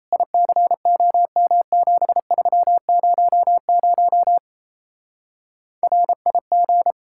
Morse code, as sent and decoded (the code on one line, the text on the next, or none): .. -.-. --- -- --... ...-- ----- ----- / .-. .. --.
ICOM7300 RIG